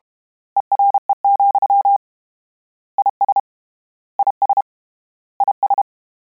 Morse code, heard as a keyed tone, about 32 wpm, 800 Hz. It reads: ERE, IS IS IS
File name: Combinations / Combinations-mp3s / Combinations@30